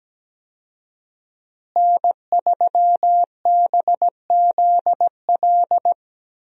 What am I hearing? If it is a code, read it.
Morse code, heard as a keyed tone, about 17 wpm, 700 Hz. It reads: N3BZL
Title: USA Call Signs - 5 Characters - Repeated 3x - 17wpm